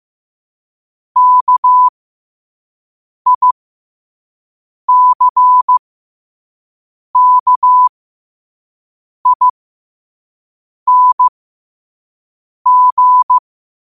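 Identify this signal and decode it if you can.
Morse code, heard as a keyed tone, about 15 wpm, spaced as for 5 wpm, 1000 Hz. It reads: KICKING
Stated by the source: Morse code sent at 5 wpm for learning.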